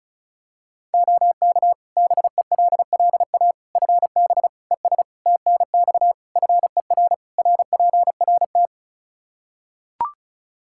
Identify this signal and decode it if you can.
Morse code, heard as a keyed tone, about 36 wpm, 700 Hz. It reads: OK BELLA FB ES TNX FER RPRT
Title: K1USN SST QSO Practice 35wpm